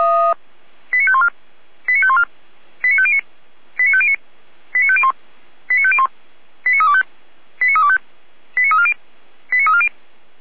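Five-tone selective call (ZVEI), heard as quick runs of five tones, tones of about 70 ms. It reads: 89514 89514 89509 89509 89501 89501 89325 89325 89340 89340
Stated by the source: ZVEI five-tone selective call 89305 received by SDR, from the audio archive of Feuerwehr Orlamünde.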